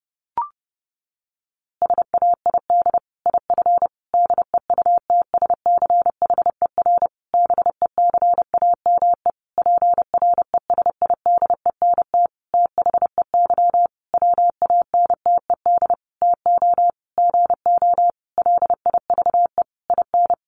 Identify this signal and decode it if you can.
Morse code, heard as a keyed tone, about 30 wpm, 700 Hz. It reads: SAID IF DEUTSCHER BECAME PRESIDENT THEY WANTED TO GO LIVE IN
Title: A Sound Of Thunder 30wpm